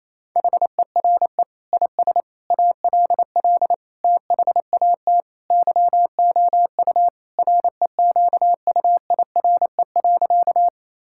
Morse code, heard as a keyed tone, about 28 wpm, 700 Hz. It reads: HERE IS ALL THAT YOU REQUIRE.
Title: Sentences from Top 1000 Words - Review Part 6 - 28wpm